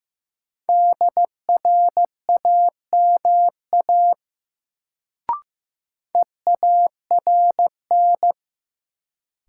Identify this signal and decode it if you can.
Morse code, heard as a keyed tone, about 15 wpm, 700 Hz. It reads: DRAMA EARN
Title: Sentences from Top 1500 Words - New Word Drill - 15wpm